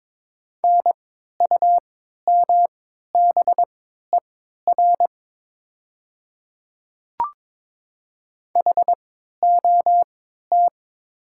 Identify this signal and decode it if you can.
Morse code, heard as a keyed tone, about 22 wpm, spaced as for 12 wpm, 700 Hz. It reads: NUMBER HOT
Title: 3x Character Spacing - Top 100 Words - 22wpm